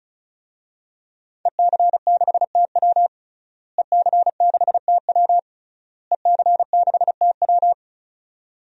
Morse code, heard as a keyed tone, about 36 wpm, 700 Hz. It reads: EC6TW EC6TW EC6TW
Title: DXCC-100 Call Signs - 5 Characters - Repeated 3x - 35wpm